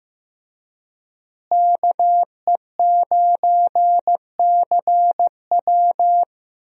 Morse code, heard as a keyed tone, about 15 wpm, 700 Hz. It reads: KE9CW
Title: USA Call Signs - 5 Characters - Repeated 3x - 15wpm